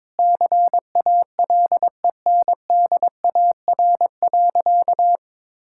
Morse code, heard as a keyed tone, about 22 wpm, 700 Hz. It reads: CALENDAR.
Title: Sentences from Top 1600 Words 22wpm